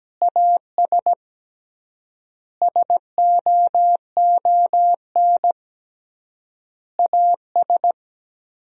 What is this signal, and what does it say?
Morse code, heard as a keyed tone, about 17 wpm, 700 Hz. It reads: AS SOON AS
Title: 3x Word-Spacing - Sentences from Top 500 Words - 17wpm